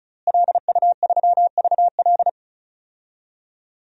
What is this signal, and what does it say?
Morse code, heard as a keyed tone, about 36 wpm, 700 Hz. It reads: LU3VL
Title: DXCC-100 Call Signs - 5 Characters - Repeated 2x - 35wpm